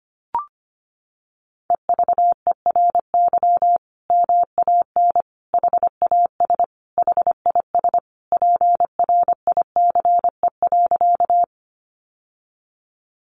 Morse code, heard as a keyed tone, about 25 wpm, 700 Hz. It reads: EVERY MAN HAS HIS PRICE.
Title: Sentences from Top 600 Words 25wpm